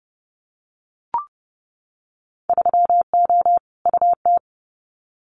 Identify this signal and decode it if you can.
Morse code, heard as a keyed tone, about 30 wpm, 700 Hz. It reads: 3O UT
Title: Winter Field Day Exchanges 30wpm